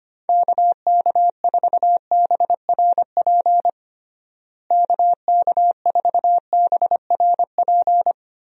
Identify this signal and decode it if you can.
Morse code, heard as a keyed tone, about 25 wpm, 700 Hz. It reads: KK4BRP KK4BRP
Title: USA Call Signs - 6 Characters - Repeated 2x - 25wpm